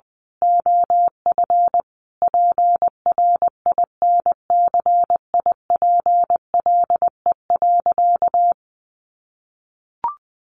Morse code, heard as a keyed tone, about 20 wpm, 700 Hz. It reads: OF PRINCIPLE.
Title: Sentences from Top 1400 Words 20wpm